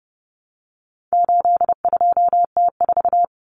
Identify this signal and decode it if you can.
Morse code, heard as a keyed tone, about 30 wpm, 700 Hz. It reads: 82T4
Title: Common Cut Numbers - 4-Digit 30wpm